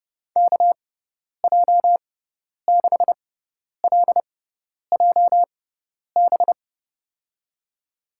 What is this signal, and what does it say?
Morse code, heard as a keyed tone, about 30 wpm, spaced as for 10 wpm, 700 Hz. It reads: KJ6LJB
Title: USA Call Signs - 6 Characters - 6x Character Spacing - 30wpm